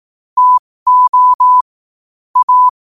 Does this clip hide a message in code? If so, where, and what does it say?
Yes: Morse code from 0.4 to 2.7 s, heard as a keyed tone, about 18 wpm, spaced as for 15 wpm, 1000 Hz. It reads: TO A